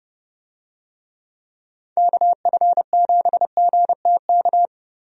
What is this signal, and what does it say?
Morse code, heard as a keyed tone, about 30 wpm, 700 Hz. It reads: KF7GTK
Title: USA Call Signs - 6 Characters - Repeated 2x - 30wpm